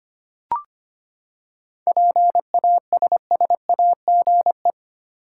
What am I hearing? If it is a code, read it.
Morse code, heard as a keyed tone, about 25 wpm, 700 Hz. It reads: PASSAGE
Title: Seven Letter Words - 25wpm